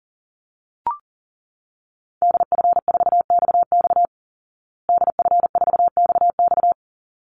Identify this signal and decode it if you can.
Morse code, heard as a keyed tone, about 40 wpm, 700 Hz. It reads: DF4XX DF4XX
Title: Call Signs Repeated 3x - 40wpm